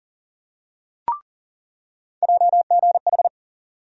Morse code, heard as a keyed tone, about 40 wpm, 700 Hz. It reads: JGH
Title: USA Call Sign Postfixes - 3 Letters 40wpm